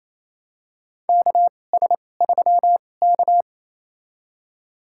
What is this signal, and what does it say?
Morse code, heard as a keyed tone, about 27 wpm, spaced as for 20 wpm, 700 Hz. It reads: KS3K
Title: USA Call Signs - 4 Characters - 2x Character Spacing - 28wpm